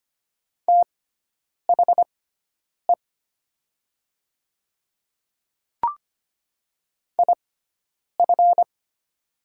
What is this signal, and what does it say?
Morse code, heard as a keyed tone, about 25 wpm, spaced as for 9 wpm, 700 Hz. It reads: THE IF
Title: 6x Character Spacing - Top 100 Words - 25wpm